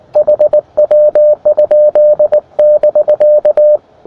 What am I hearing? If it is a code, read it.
Morse code, heard as a keyed tone, about 19 wpm, 600 Hz. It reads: HW? <BK>
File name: attachments / rd7k 20m